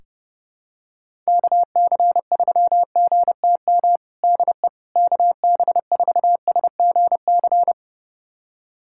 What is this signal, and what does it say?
Morse code, heard as a keyed tone, about 30 wpm, 700 Hz. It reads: KC3GTM DE KB4SGC
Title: USA Callsign DE Callsign - 6 Characters - 30wpm